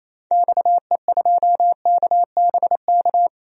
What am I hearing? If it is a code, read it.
Morse code, heard as a keyed tone, about 28 wpm, 700 Hz. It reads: XE2KBK
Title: DXCC-100 Call Signs - 6 Characters - 28wpm